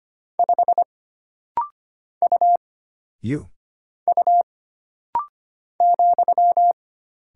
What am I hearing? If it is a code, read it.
Morse code, heard as a keyed tone, about 25 wpm, 700 Hz. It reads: 5 U U ,
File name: Single Character (Letter:Number:Punctuation:Prosign) - Rapid-Fire - 25wpm